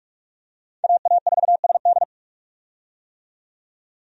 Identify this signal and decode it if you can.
Morse code, heard as a keyed tone, about 43 wpm, 700 Hz. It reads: AA4SD
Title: USA Call Signs - 5 Characters - 45wpm